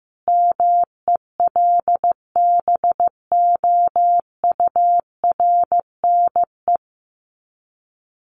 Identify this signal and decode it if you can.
Morse code, heard as a keyed tone, about 15 wpm, 700 Hz. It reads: MELBOURNE
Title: Common QSO Phrases 15wpm